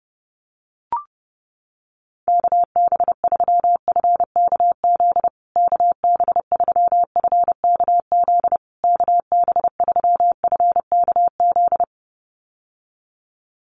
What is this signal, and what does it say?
Morse code, heard as a keyed tone, about 30 wpm, 700 Hz. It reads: KB3FKZ KB3FKZ KB3FKZ
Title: USA Call Signs Repeated 3x 30wpm